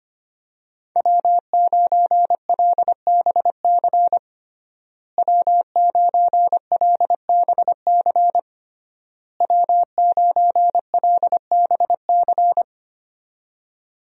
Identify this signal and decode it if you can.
Morse code, heard as a keyed tone, about 25 wpm, 700 Hz. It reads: W9LBC W9LBC W9LBC
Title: USA Call Signs - 5 Characters - Repeated 3x - 25wpm